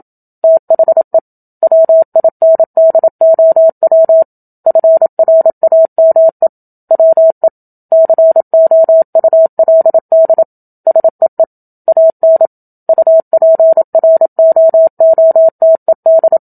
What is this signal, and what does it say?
Morse code, heard as a keyed tone, about 27 wpm, 650 Hz. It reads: THE WINDOW FRAME WE COULD SEE AN UPROOTED
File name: Book Two / 01-Chapter 1 - 27 WPM